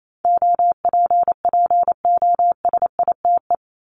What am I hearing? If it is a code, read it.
Morse code, heard as a keyed tone, about 28 wpm, 700 Hz. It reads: OPPOSITE
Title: Top 1251-1500 Words 28wpm